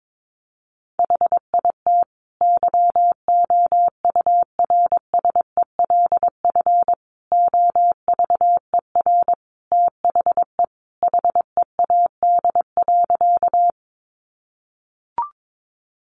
Morse code, heard as a keyed tone, about 22 wpm, 700 Hz. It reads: HIT YOURSELF OVER THE HEAD.
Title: Sentences from Top 1200 Words 22wpm